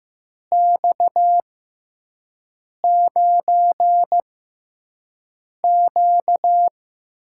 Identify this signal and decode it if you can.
Morse code, heard as a keyed tone, about 15 wpm, spaced as for 5 wpm, 700 Hz. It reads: X9Q
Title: USA Call Signs - 6x Character Spacing - 15wpm